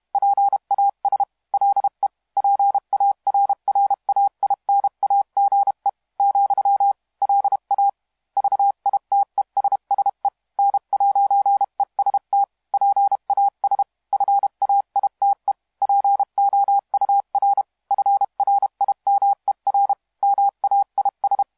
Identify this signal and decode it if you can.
Morse code, heard as a keyed tone, about 32 wpm, 800 Hz. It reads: PAS LE PARRAINAGE , LA VITESSE N'EST PAS FAITE POUR FRIMER MAIS
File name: musiqueradio / cw hsc demo f6ddr